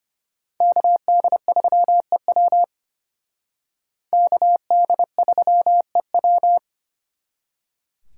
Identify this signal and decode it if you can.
Morse code, heard as a keyed tone, about 31 wpm, 700 Hz. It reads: KD3EW KD3EW